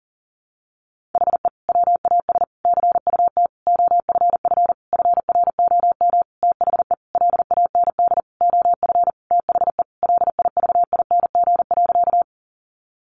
Morse code, heard as a keyed tone, about 40 wpm, 700 Hz. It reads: HE WAS CUT OFF FROM THE LAND OF THE LIVING.
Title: Sentences from Top 1100 Words 40wpm